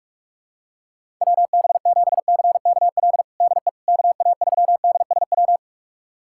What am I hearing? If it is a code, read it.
Morse code, heard as a keyed tone, about 46 wpm, 700 Hz. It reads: WB7CKL DE KA3DSW
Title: USA Callsign DE Callsign - 6 Characters - 45wpm